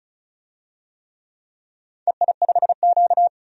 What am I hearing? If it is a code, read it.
Morse code, heard as a keyed tone, about 36 wpm, 700 Hz. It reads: EI5Q
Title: DXCC-100 Call Signs - 4 Characters - 35wpm